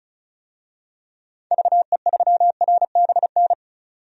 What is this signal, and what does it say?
Morse code, heard as a keyed tone, about 36 wpm, 700 Hz. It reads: VE3RBN
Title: DXCC-100 Call Signs - 6 Characters - 35wpm